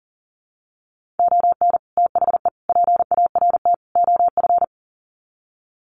Morse code, heard as a keyed tone, about 37 wpm, 700 Hz. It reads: ON THE PART OF
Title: Sets of 4 Words 40wpm